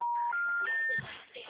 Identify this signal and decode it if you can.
Special information tone, with speech.
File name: out- -2005- - - .109